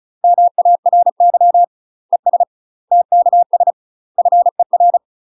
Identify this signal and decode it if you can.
Morse code, heard as a keyed tone, about 36 wpm, 700 Hz. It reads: MARY ES TKS FER
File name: UK MASTER QSOs TRAINING / 35WPM / UK MASTER QSOs 0031